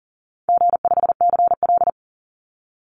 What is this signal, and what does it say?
Morse code, heard as a keyed tone, about 38 wpm, 700 Hz. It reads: G5CL
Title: Easy Call Signs - 40wpm